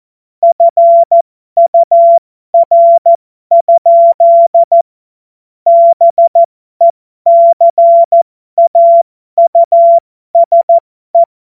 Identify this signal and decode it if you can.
Morse code, heard as a keyed tone, about 14 wpm, spaced as for 12 wpm, 680 Hz. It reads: FUR? BECAUSE
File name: Morsekurs / Witze